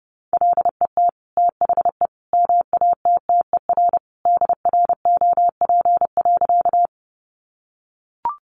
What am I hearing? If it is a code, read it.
Morse code, heard as a keyed tone, about 30 wpm, 700 Hz. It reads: LET THE MATTER DROP.
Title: Sentences from Top 1400 Words 30wpm